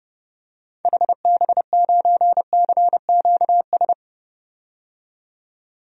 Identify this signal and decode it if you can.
Morse code, heard as a keyed tone, about 32 wpm, 700 Hz. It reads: HB9CQS
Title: DXCC-100 Call Signs - 6 Characters - 30wpm